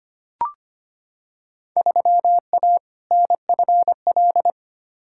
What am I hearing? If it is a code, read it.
Morse code, heard as a keyed tone, about 25 wpm, 700 Hz. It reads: 3A NFL
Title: Field Day Exchanges - 25wpm